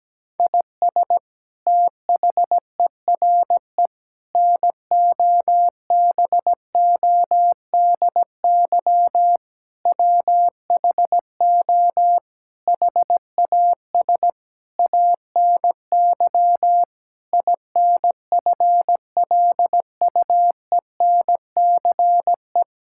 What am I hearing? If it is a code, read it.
Morse code, heard as a keyed tone, about 17 wpm, 700 Hz. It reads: IS THERE NOBODY WHO HAS ANY INFLUENCE
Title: Sentences from Top 1600 Words 17wpm